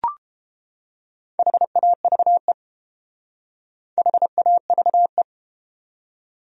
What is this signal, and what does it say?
Morse code, heard as a keyed tone, about 29 wpm, 700 Hz. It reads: HAVE HAVE